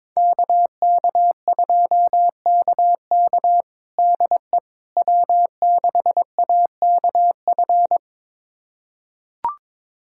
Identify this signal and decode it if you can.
Morse code, heard as a keyed tone, about 22 wpm, 700 Hz. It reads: KK2KK DE W6AKF
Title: USA Callsign DE Callsign - 5 Characters - 22wpm